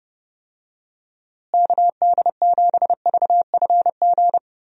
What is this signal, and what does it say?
Morse code, heard as a keyed tone, about 30 wpm, 700 Hz. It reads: KD7VFG
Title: USA Call Signs - 6 Characters - Repeated 2x - 30wpm